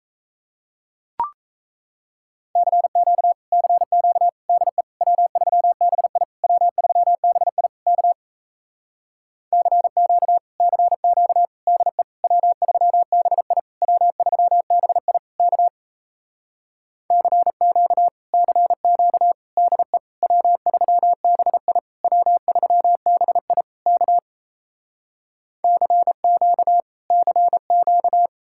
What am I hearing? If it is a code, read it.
Morse code, heard as a keyed tone, about 38 wpm, 700 Hz. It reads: CQ CQ DE W3BI W3BI K CQ CQ DE W3BI W3BI K CQ CQ DE W3BI W3BI K CQ CQ